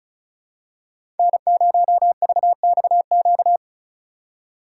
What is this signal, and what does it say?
Morse code, heard as a keyed tone, about 34 wpm, 700 Hz. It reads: N0VXQ
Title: USA Call Signs - 5 Characters - 2x Character Spacing - 35wpm